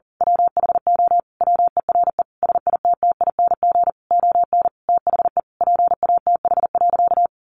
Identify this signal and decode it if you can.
Morse code, heard as a keyed tone, about 40 wpm, 700 Hz. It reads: WHO WERE SITTING ON THE PATH.
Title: A Sound Of Thunder 40wpm